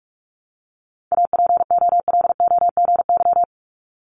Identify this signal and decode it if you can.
Morse code, heard as a keyed tone, about 46 wpm, 700 Hz. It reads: APOLOGY
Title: Multi-letter Phonemes 45wpm